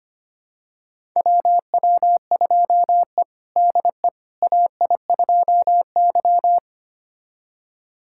Morse code, heard as a keyed tone, about 25 wpm, 700 Hz. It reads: WW2E DE AI2Y
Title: USA Callsign DE Callsign - 4 Characters - 25wpm